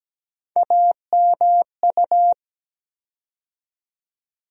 Morse code, heard as a keyed tone, about 17 wpm, 700 Hz. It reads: AMU
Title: USA Call Sign Postfixes - 3 Letters 17wpm